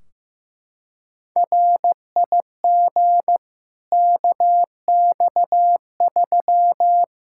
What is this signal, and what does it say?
Morse code, heard as a keyed tone, about 15 wpm, 700 Hz. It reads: RIG KX3
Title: Practice QSO - Exchange 2 - Info - 15wpm